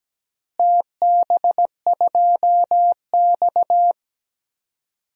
Morse code, heard as a keyed tone, about 17 wpm, 700 Hz. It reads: TB2X
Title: DXCC-100 Call Signs - 4 Characters - Repeated 2x - 17wpm